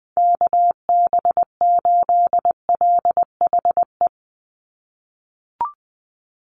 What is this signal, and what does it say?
Morse code, heard as a keyed tone, about 20 wpm, 700 Hz. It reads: KB8LHE
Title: USA Call Signs 20wpm